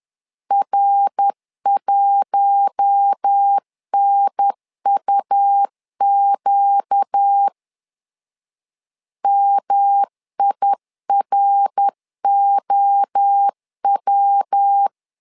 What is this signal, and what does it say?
Morse code, heard as a keyed tone, about 10 wpm, 800 Hz. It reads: R1NUQ MIROW
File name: LEZIONI / 4 Q1 10wpm 2448